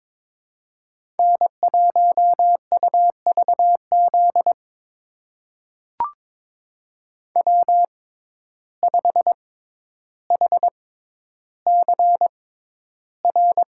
Morse code, heard as a keyed tone, about 22 wpm, spaced as for 11 wpm, 700 Hz. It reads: N1UVZ W5HCR
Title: USA Call Signs - 5 Characters - 6x Character Spacing - 22wpm